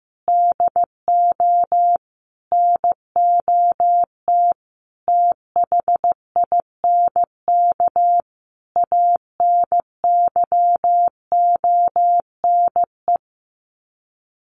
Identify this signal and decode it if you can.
Morse code, heard as a keyed tone, about 15 wpm, 700 Hz. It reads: DO NOT THINK ANYONE
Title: Sets of 5 Words Encore 15wpm